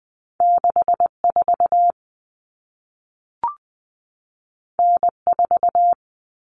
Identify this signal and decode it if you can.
Morse code, heard as a keyed tone, about 20 wpm, 700 Hz. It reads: 64 N4
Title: Common Cut Numbers - 2-Digit 20wpm